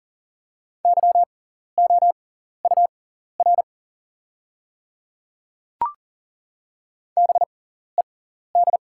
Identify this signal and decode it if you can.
Morse code, heard as a keyed tone, about 41 wpm, spaced as for 14 wpm, 690 Hz. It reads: YOUR BED
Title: 6x Character Spacing - Top 500 Words - 40wpm